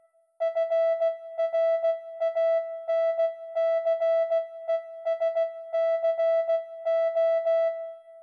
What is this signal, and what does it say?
Morse code, heard as a keyed tone, about 16 wpm, 660 Hz. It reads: FRANCESCO